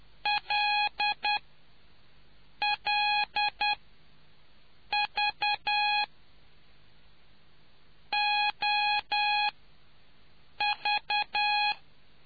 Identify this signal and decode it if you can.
Morse code, heard as a keyed tone, about 10 wpm, 830 Hz. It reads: L L V O V